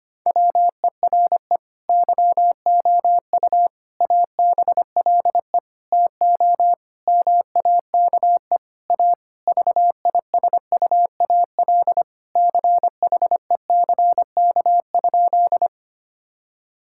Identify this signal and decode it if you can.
Morse code, heard as a keyed tone, about 25 wpm, 700 Hz. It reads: WERE YOU ABLE TO MAKE A VISUAL CHECK?
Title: Sentences from Top 1000 Words - Review Part 6 - 25wpm